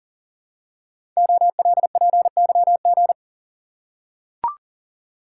Morse code, heard as a keyed tone, about 40 wpm, 700 Hz. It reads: OLPYG
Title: Code Groups - 5 Letters - 40wpm